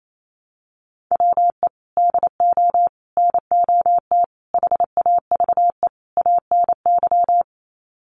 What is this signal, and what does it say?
Morse code, heard as a keyed tone, about 28 wpm, 700 Hz. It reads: WE DO NOT HAVE ANY
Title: Sets of 5 Words 28wpm